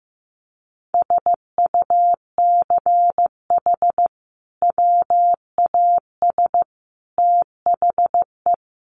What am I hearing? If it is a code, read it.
Morse code, heard as a keyed tone, about 14 wpm, 700 Hz. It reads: SUCH WAS THE
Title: Sentences from Top 1200 Words 15wpm